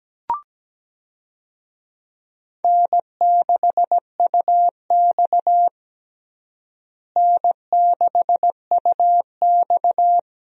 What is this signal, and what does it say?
Morse code, heard as a keyed tone, about 17 wpm, 700 Hz. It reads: N6UX N6UX
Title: USA Call Signs - 4 Characters - Repeated 2x - 17wpm